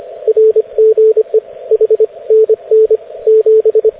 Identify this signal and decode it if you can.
Morse code, heard as a keyed tone, about 25 wpm, 440 Hz, with radio noise. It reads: RGE H NN 7
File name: cw / ubacw dx